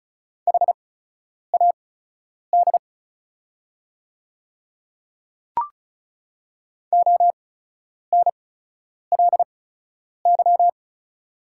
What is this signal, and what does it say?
Morse code, heard as a keyed tone, about 36 wpm, spaced as for 10 wpm, 690 Hz. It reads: HAD ONLY
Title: 8x Character Spacing - Top 200 Words - 35wpm